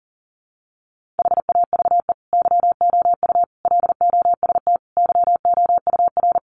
Morse code, heard as a keyed tone, about 40 wpm, 700 Hz. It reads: HAVE YOU LOST YOUR
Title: Sentences from Top 1400 Words 40wpm